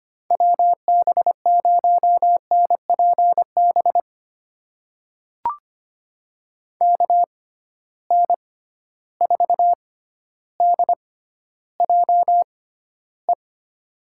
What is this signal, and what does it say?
Morse code, heard as a keyed tone, about 25 wpm, spaced as for 13 wpm, 700 Hz. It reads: WB0NPB KN4DJE
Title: USA Call Signs - 6 Characters - 6x Character Spacing - 25wpm